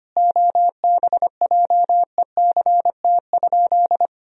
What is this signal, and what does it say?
Morse code, heard as a keyed tone, about 25 wpm, 700 Hz. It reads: OBJECT?
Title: Sentences from Top 500 Words and Abbreviations 25wpm